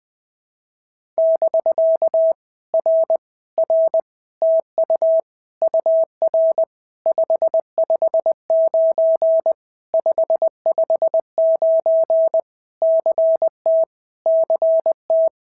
Morse code, heard as a keyed tone, about 20 wpm, 650 Hz. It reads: <BK> R R TU UR 559 559 CT CT
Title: POTA QSO Practice - 20wpm